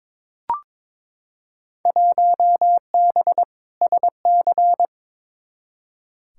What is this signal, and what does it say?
Morse code, heard as a keyed tone, about 22 wpm, 700 Hz. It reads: 1B SC
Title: Field Day Exchanges - 22wpm